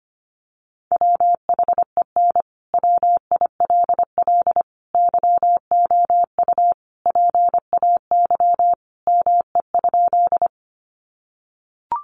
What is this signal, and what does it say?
Morse code, heard as a keyed tone, about 25 wpm, 700 Hz. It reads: WHEN WILL YOU PAY ME?
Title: Sentences from Top 700 Words 25wpm